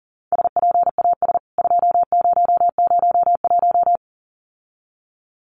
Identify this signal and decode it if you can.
Morse code, heard as a keyed tone, about 37 wpm, 700 Hz. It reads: SPAS 2001
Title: CWT Exchange 40wpm